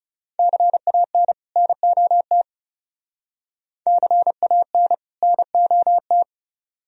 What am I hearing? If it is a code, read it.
Morse code, heard as a keyed tone, about 38 wpm, 700 Hz. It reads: CAN NOT CAN NOT